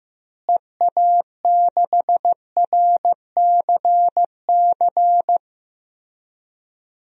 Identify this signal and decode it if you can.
Morse code, heard as a keyed tone, about 15 wpm, 700 Hz. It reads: EA6RCC
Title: DXCC-100 Call Signs - 6 Characters - Repeated 3x - 15wpm